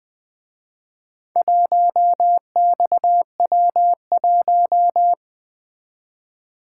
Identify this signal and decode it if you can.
Morse code, heard as a keyed tone, about 23 wpm, 700 Hz. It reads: 1XW1
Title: Speed-Racing - Four Letter-Number Combinations - 20wpm